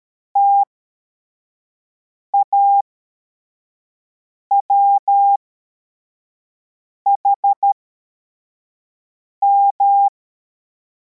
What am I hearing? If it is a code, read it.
Morse code, heard as a keyed tone, about 13 wpm, spaced as for 4 wpm, 800 Hz. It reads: TAWHM
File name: formation / cw / Text74